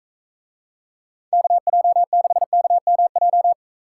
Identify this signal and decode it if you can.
Morse code, heard as a keyed tone, about 44 wpm, 700 Hz. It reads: KJ6KMJ